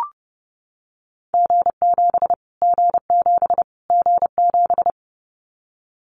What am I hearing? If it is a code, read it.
Morse code, heard as a keyed tone, about 30 wpm, 700 Hz. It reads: G7 G7 G7
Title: Callsign Prefixes for Top 25 DXCC - Repeated 3x - 30wpm